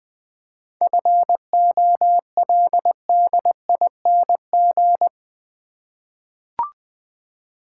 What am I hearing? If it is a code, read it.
Morse code, heard as a keyed tone, about 20 wpm, 700 Hz. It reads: FOLDING
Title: 7 Letter Words - First 3 Letters Given - 20wpm